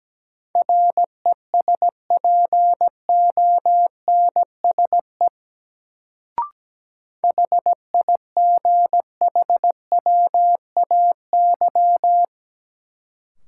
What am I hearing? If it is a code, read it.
Morse code, heard as a keyed tone, about 17 wpm, 700 Hz. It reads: RESPONSE HIGHWAY
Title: Sentences from Top 1400 Words - New Word Drill - 17wpm